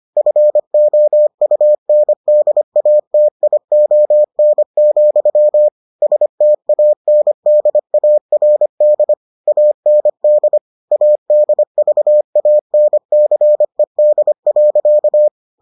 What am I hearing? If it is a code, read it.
Morse code, heard as a keyed tone, about 25 wpm, 600 Hz. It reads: FOUNDATION, STANDARD AND ADVANCED.